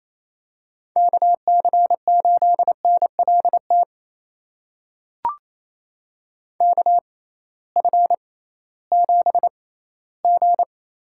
Morse code, heard as a keyed tone, about 28 wpm, spaced as for 16 wpm, 700 Hz. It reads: KC8NLT KF7G
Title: USA Call Signs - 6x Character Spacing - 28wpm